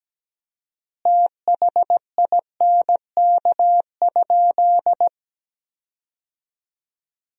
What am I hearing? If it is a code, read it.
Morse code, heard as a keyed tone, about 17 wpm, 700 Hz. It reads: THINK?
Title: 4x Word-Spacing - Sentences from Top 200 Words - 17wpm